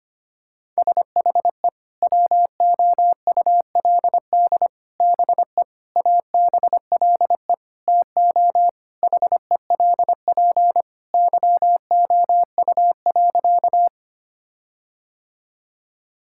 Morse code, heard as a keyed tone, about 25 wpm, 700 Hz. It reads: SHE WOULD BE ABLE TO HELP YOU.
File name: practice / 25wpm / Sentences with 4-Word Phrases - 25wpm